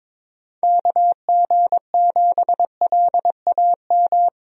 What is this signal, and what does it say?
Morse code, heard as a keyed tone, about 22 wpm, 700 Hz. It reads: KG7LAM
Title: USA Call Signs - 6 Characters - Repeated 3x - 22wpm